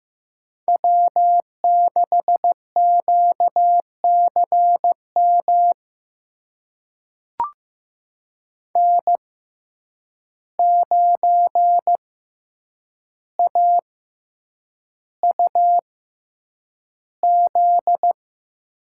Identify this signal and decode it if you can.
Morse code, heard as a keyed tone, about 15 wpm, spaced as for 8 wpm, 700 Hz. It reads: W6QCM N9AUZ
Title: USA Call Signs - 5 Characters - 6x Character Spacing - 15wpm